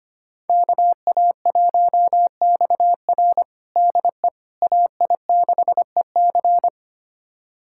Morse code, heard as a keyed tone, about 25 wpm, 700 Hz. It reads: KA1XR DE AI6EC
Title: USA Callsign DE Callsign - 5 Characters - 25wpm